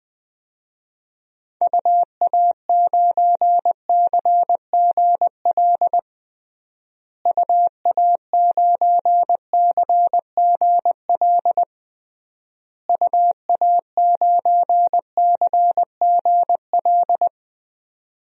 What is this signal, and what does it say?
Morse code, heard as a keyed tone, about 20 wpm, 700 Hz. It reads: UA9CGL UA9CGL UA9CGL
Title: DXCC-100 Call Signs - 6 Characters - Repeated 3x - 20wpm